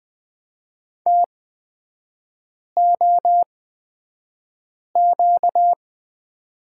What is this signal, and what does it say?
Morse code, heard as a keyed tone, about 20 wpm, spaced as for 5 wpm, 700 Hz. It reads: TOQ
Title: 236 Lesson 38e - Letter q - Three Characters 20wpm